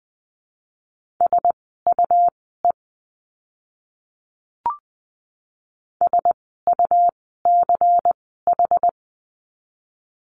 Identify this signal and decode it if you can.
Morse code, heard as a keyed tone, about 20 wpm, spaced as for 15 wpm, 700 Hz. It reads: SUE SUCH
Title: 101 Lesson 17f - Letter u - Words 20wpm